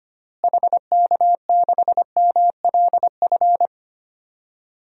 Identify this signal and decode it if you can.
Morse code, heard as a keyed tone, about 25 wpm, 700 Hz. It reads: HK6MLF
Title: DXCC-100 Call Signs - 6 Characters - Repeated 2x - 25wpm